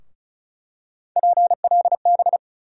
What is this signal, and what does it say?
Morse code, heard as a keyed tone, about 34 wpm, 700 Hz. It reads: PLB
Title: USA Call Sign Postfixes - 3 Letters 35wpm